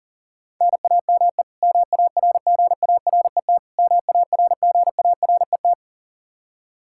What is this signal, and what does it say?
Morse code, heard as a keyed tone, about 40 wpm, 700 Hz. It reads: NAME MARGARET MARGARET
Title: K1USN SST QSO Practice 40wpm